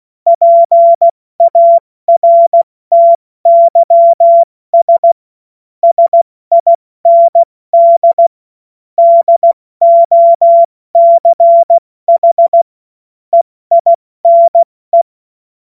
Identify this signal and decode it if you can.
Morse code, heard as a keyed tone, about 16 wpm, 680 Hz. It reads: PARTYS SIND DOCH EINE
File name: Morsekurs / Witze / Witz 08. Februar 2022 70 F80